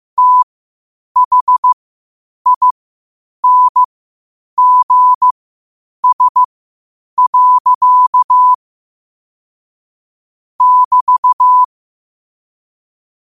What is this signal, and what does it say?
Morse code, heard as a keyed tone, about 15 wpm, spaced as for 8 wpm, 1000 Hz. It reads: THINGS. <BT>
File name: qotdfiles / 8wpm